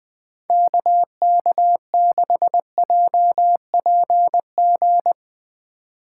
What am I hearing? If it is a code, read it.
Morse code, heard as a keyed tone, about 20 wpm, 700 Hz. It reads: KK6JPG
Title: Speed Racing - USA Call Signs 17wpm